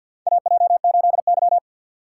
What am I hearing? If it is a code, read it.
Morse code, heard as a keyed tone, about 47 wpm, 700 Hz. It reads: AJ8Y
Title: USA Call Signs - 4 Characters - 2x Character Spacing - 50wpm